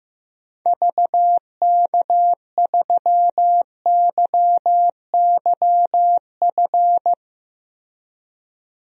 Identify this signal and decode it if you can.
Morse code, heard as a keyed tone, about 15 wpm, 700 Hz. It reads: VK3YYF
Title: DXCC-100 Call Signs - 6 Characters - 15wpm